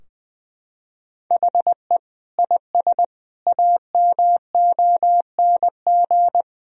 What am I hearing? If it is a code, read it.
Morse code, heard as a keyed tone, about 20 wpm, 700 Hz. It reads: HE IS AMONG
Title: Sentences from Top 1000 Words - Review Part 3 - 20wpm